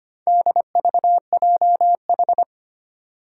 Morse code, heard as a keyed tone, about 25 wpm, 700 Hz. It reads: DVJH
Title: Four Challenging Letter-Number Combinations 25wpm